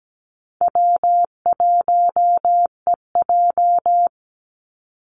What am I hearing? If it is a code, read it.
Morse code, heard as a keyed tone, about 17 wpm, 700 Hz. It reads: W1EJ
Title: NA SOTA Activator Callsigns 17wpm